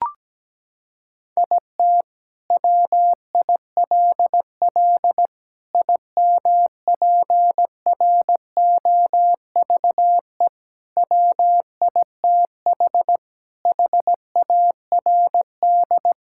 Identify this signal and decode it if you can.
Morse code, heard as a keyed tone, about 17 wpm, 700 Hz. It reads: IT WILL IMPROVE WITH HARD
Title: Sentences from Top 1000 Words - Review Part 5 - 17wpm